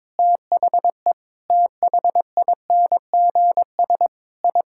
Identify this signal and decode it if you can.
Morse code, heard as a keyed tone, about 22 wpm, 700 Hz. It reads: THE THINGS I
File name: practice / 22wpm / Sentences with 3-Word Phrases - 22wpm